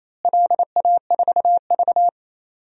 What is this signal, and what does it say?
Morse code, heard as a keyed tone, about 28 wpm, 700 Hz. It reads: LA4V
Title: DXCC-100 Call Signs - 4 Characters - Repeated 2x - 28wpm